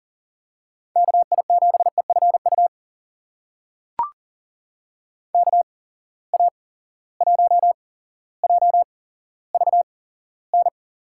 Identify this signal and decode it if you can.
Morse code, heard as a keyed tone, about 40 wpm, spaced as for 17 wpm, 700 Hz. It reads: KI7EFU KA1JVN
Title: USA Call Signs - 6 Characters - 8x Character Spacing - 40wpm